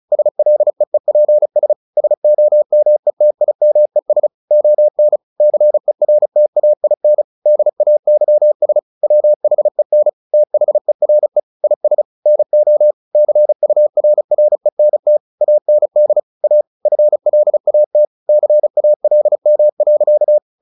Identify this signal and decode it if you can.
Morse code, heard as a keyed tone, about 34 wpm, 600 Hz. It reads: SLEEPS SOMETIMES ON CERTAIN DAYS WHEN THERE IS NO CURRENT AND A FLAT CALM.
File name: the old man and the sea 2 35wpm